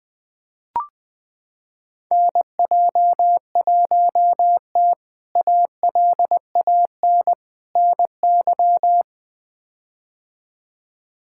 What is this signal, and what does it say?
Morse code, heard as a keyed tone, about 20 wpm, 700 Hz. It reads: NJ1T ALAN NY
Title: K1USN SST QSO Practice - 20wpm